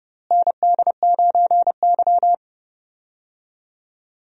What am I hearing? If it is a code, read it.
Morse code, heard as a keyed tone, about 30 wpm, 700 Hz. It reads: ND9Y
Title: USA Call Signs - 4 Characters - Repeated 3x - 30wpm